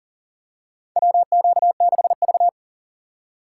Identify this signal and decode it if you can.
Morse code, heard as a keyed tone, about 40 wpm, 700 Hz. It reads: WQ6V
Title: USA Call Signs - 4x Character Spacing - 40wpm